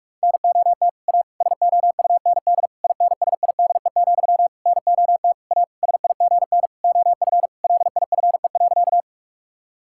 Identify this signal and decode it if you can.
Morse code, heard as a keyed tone, about 46 wpm, 700 Hz. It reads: NOT A SOUND INSIDE, NOT A SIGN OF LIFE.
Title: Sentences from Top 1000 Words - Review Part 6 - 45wpm